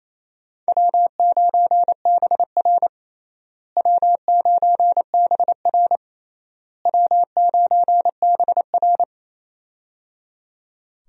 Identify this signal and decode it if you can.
Morse code, heard as a keyed tone, about 28 wpm, 700 Hz. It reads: W9BR W9BR W9BR
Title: USA Call Signs - 4 Characters - Repeated 3x - 28wpm